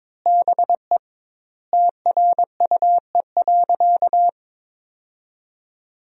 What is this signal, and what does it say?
Morse code, heard as a keyed tone, about 22 wpm, 700 Hz. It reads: BE TRUE.
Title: 2x Word-Spacing - Sentences from Top 400 Words - 22wpm